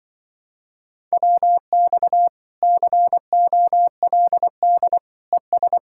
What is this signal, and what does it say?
Morse code, heard as a keyed tone, about 24 wpm, 700 Hz. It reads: WX COLD ES
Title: Speed-Racing Common QSO Phrases - 20wpm